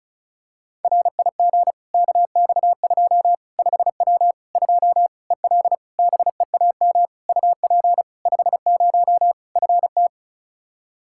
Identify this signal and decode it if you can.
Morse code, heard as a keyed tone, about 36 wpm, 700 Hz. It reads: RIG KX2 5W 2 EL BEAM UP 50 FT
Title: Practice QSO - Exchange 2 - Info - 35wpm